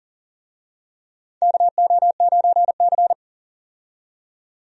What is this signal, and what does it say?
Morse code, heard as a keyed tone, about 40 wpm, 700 Hz. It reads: KO9C
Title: USA Call Signs - 4 Characters - 40wpm